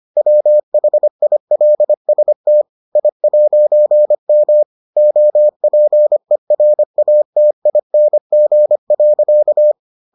Morse code, heard as a keyed tone, about 25 wpm, 600 Hz. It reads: WHILST I'M OPERATING.